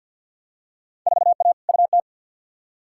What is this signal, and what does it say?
Morse code, heard as a keyed tone, about 51 wpm, 700 Hz. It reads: 4A UT
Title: Field Day Exchanges - 50wpm